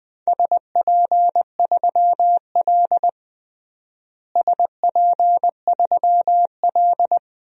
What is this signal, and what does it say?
Morse code, heard as a keyed tone, about 21 wpm, 700 Hz. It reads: SP3L SP3L
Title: DXCC-100 Call Signs - 4 Characters - Repeated 2x - 20wpm